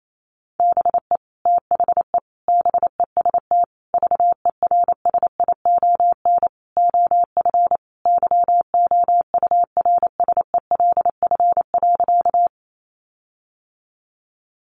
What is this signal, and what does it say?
Morse code, heard as a keyed tone, about 28 wpm, 700 Hz. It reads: BE THE BEST VERSION OF YOURSELF.
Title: Sentences from Top 1400 Words 28wpm